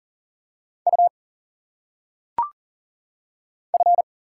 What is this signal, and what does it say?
Morse code, heard as a keyed tone, about 40 wpm, 700 Hz. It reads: U F
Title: Single Letters - Phonetic Pronunciation 40wpm